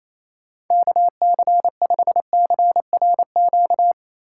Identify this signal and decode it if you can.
Morse code, heard as a keyed tone, about 27 wpm, 700 Hz. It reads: KC5CRQ
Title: USA Call Signs - 6 Characters - 4x Character Spacing - 28wpm